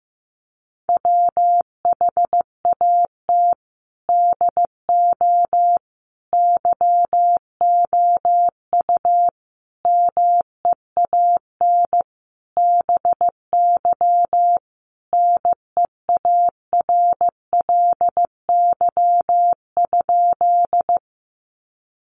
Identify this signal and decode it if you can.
Morse code, heard as a keyed tone, about 15 wpm, 700 Hz. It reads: WHAT DO YOU MEAN BY NEARLY?
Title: Sentences from Top 1300 Words 15wpm